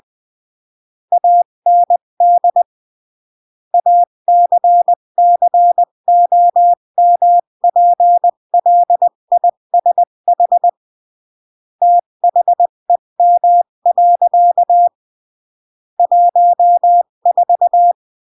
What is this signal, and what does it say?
Morse code, heard as a keyed tone, about 20 wpm, 700 Hz. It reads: AND ACCOMPLISH THEM. 14
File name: CW / jodorowsky / 82 commandments 20-15